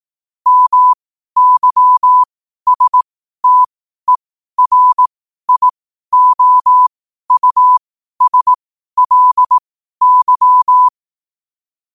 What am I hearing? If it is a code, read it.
Morse code, heard as a keyed tone, about 18 wpm, spaced as for 12 wpm, 1000 Hz. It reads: MYSTERIOUSLY
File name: qotdfiles / 12wpm